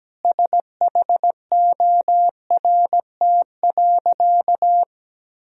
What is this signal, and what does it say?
Morse code, heard as a keyed tone, about 17 wpm, 700 Hz. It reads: SHORT.
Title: Sentences from Top 1000 Words - Review Part 2 - 17wpm